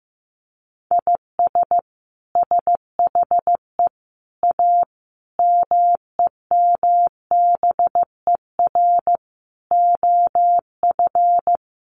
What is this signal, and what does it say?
Morse code, heard as a keyed tone, about 15 wpm, 700 Hz. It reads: IS SHE A MEMBER OF
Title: Sentences from Top 1100 Words 15wpm